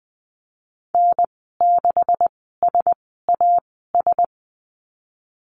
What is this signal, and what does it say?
Morse code, heard as a keyed tone, about 20 wpm, spaced as for 15 wpm, 700 Hz. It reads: N6SAS
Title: 154 Lesson 25g - Letter p - Call Signs 20wpm